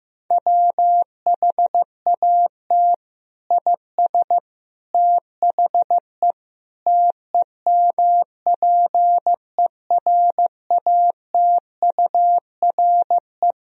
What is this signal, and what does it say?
Morse code, heard as a keyed tone, about 15 wpm, 700 Hz. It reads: WHAT IS THE TEMPERATURE
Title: Sentences from Top 1000 Words - Review Part 4 - 15wpm